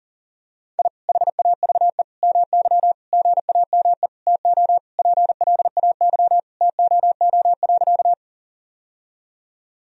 Morse code, heard as a keyed tone, about 40 wpm, 700 Hz. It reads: I HAVE MY GAME TO PLAY TOO.
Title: Sentences from Top 1000 Words - Review Part 3 - 40wpm